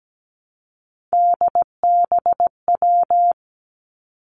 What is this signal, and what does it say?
Morse code, heard as a keyed tone, about 17 wpm, 700 Hz. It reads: DBW
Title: Three Challenging Letter-Number Combinations 17wpm